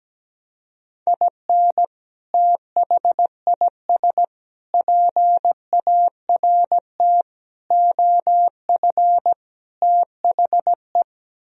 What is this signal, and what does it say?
Morse code, heard as a keyed tone, about 17 wpm, 700 Hz. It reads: IN THIS PART OF THE
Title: Speed-Racing - Sets of 5 Words 17wpm